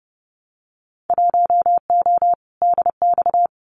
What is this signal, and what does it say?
Morse code, heard as a keyed tone, about 30 wpm, 700 Hz. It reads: 1O DX
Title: Winter Field Day Exchanges 30wpm